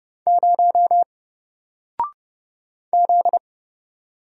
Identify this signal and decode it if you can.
Morse code, heard as a keyed tone, about 30 wpm, 700 Hz. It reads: 0 Z
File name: Single-Letter-Number-30wpm